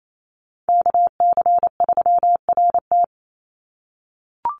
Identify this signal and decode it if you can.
Morse code, heard as a keyed tone, about 28 wpm, 700 Hz. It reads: KC3RT
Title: NA SOTA Chaser Callsigns 28wpm